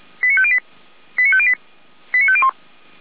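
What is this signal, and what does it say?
Five-tone selective call (ZVEI), heard as quick runs of five tones, tones of about 70 ms. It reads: 89508 89508 89501
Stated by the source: ZVEI five-tone selective call 89310 received by SDR.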